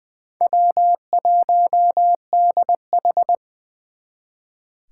Morse code, heard as a keyed tone, about 24 wpm, 700 Hz. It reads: W1DH
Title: Speed Racing - USA 1x2 Call Signs 20wpm